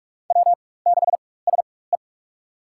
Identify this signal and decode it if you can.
Morse code, heard as a keyed tone, about 41 wpm, spaced as for 20 wpm, 700 Hz. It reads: W6SE
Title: USA Call Signs - 4 Characters - 4x Character Spacing - 45wpm